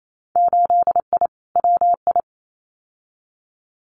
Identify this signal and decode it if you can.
Morse code, heard as a keyed tone, about 28 wpm, 700 Hz. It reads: 8I WI
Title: Winter Field Day Exchanges 28wpm